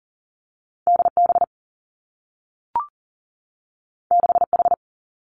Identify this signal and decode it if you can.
Morse code, heard as a keyed tone, about 39 wpm, 700 Hz. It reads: D B 6 H
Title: Two Challenging Letter-Number Combinations 40wpm